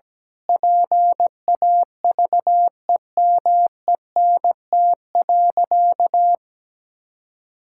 Morse code, heard as a keyed tone, about 17 wpm, 700 Hz. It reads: PAVEMENT.
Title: Pangram Challenge v2 - 17wpm